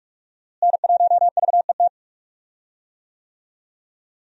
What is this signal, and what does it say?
Morse code, heard as a keyed tone, about 46 wpm, 700 Hz. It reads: N1VET
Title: USA Call Signs - 5 Characters - Repeated 3x - 45wpm